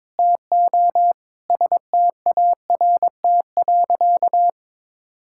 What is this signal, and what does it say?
Morse code, heard as a keyed tone, about 22 wpm, 700 Hz. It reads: TO START.
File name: Sentences with 9-Word Phrases - 22wpm